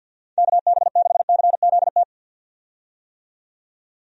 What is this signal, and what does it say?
Morse code, heard as a keyed tone, about 51 wpm, 700 Hz. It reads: KB6CZT
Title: USA Call Signs - 6 Characters - 50wpm